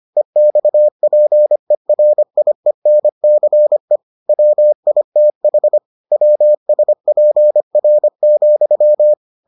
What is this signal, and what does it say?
Morse code, heard as a keyed tone, about 25 wpm, 600 Hz. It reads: EXPERIENCE WITH WSPR,